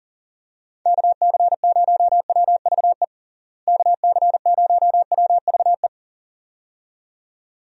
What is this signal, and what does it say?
Morse code, heard as a keyed tone, about 40 wpm, 700 Hz. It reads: KC0WVE KC0WVE
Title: USA Call Signs - 6 Characters - Repeated 2x - 40wpm